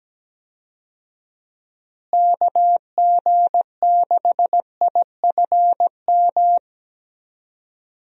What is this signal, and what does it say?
Morse code, heard as a keyed tone, about 17 wpm, 700 Hz. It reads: KG6IFM
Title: USA Call Signs - 6 Characters - Repeated 2x - 17wpm